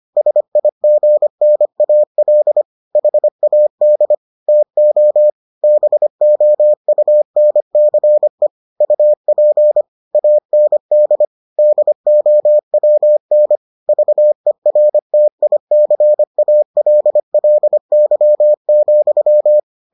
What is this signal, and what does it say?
Morse code, heard as a keyed tone, about 25 wpm, 600 Hz. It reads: SIGNAL HAD TO BOUNCE UP AND DOWN VERTICALLY,